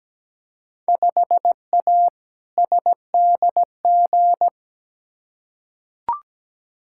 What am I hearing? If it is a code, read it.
Morse code, heard as a keyed tone, about 17 wpm, 700 Hz. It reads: 5A SDG
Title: Field Day Exchanges Encore - 17wpm